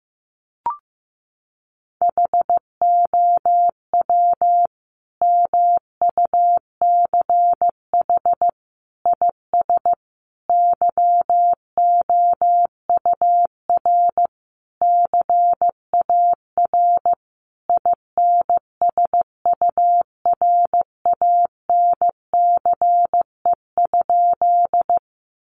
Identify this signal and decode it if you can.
Morse code, heard as a keyed tone, about 15 wpm, 700 Hz. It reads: HOW MUCH IS YOUR CAR INSURANCE?
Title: Sentences from Top 1100 Words 15wpm